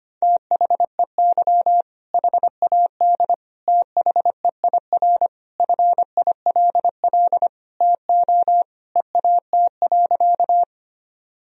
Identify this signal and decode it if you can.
Morse code, heard as a keyed tone, about 25 wpm, 700 Hz. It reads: THEY HAD THEIR FILL TO EAT.
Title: Sentences from Top 1000 Words - Review Part 3 - 25wpm